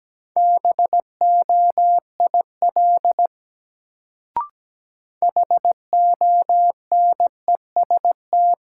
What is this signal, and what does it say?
Morse code, heard as a keyed tone, about 17 wpm, 700 Hz. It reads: BOIL HONEST
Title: Sets of 5 Words Encore - New Word Drill - 17wpm